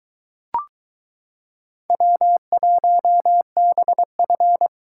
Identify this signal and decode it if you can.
Morse code, heard as a keyed tone, about 22 wpm, 700 Hz. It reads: W1BF